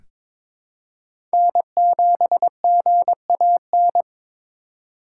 Morse code, heard as a keyed tone, about 22 wpm, 700 Hz. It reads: N7GAN
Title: USA Call Signs - 5 Characters - 2x Character Spacing - 22wpm